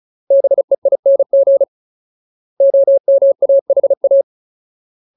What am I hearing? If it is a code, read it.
Morse code, heard as a keyed tone, about 34 wpm, 550 Hz. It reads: BEING OMAHA
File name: Ultimate Word List / 35WPM Ultimate